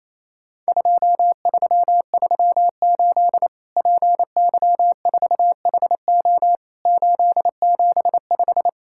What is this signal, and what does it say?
Morse code, heard as a keyed tone, about 27 wpm, 700 Hz. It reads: 2338 PY4HO 875
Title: Worked All Europe DX Contest - QTC Exchange - 28wpm